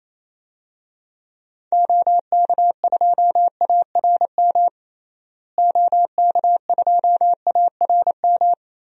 Morse code, heard as a keyed tone, about 28 wpm, 700 Hz. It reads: OK2ARM OK2ARM
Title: DXCC-100 Call Signs - 6 Characters - Repeated 3x - 28wpm